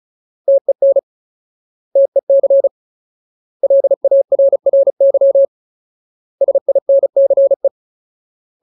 Morse code, heard as a keyed tone, about 34 wpm, 550 Hz. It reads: TEN TEC LARRY SINCE
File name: Ultimate Word List / 35WPM Ultimate